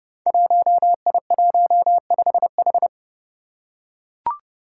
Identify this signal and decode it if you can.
Morse code, heard as a keyed tone, about 30 wpm, 700 Hz. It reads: 1I15H
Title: Five Challenging Letter-Number Combinations 30wpm